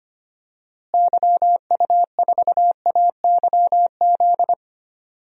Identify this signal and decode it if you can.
Morse code, heard as a keyed tone, about 25 wpm, 700 Hz. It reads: YU4AYZ
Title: DXCC-100 Call Signs - 6 Characters - Repeated 3x - 25wpm